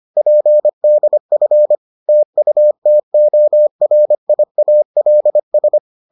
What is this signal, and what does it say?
Morse code, heard as a keyed tone, about 25 wpm, 600 Hz. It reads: PDF TUTORIALS